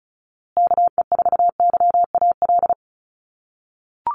Morse code, heard as a keyed tone, about 34 wpm, 700 Hz. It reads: KE4YAL
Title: USA Call Signs Repeated 3x 35wpm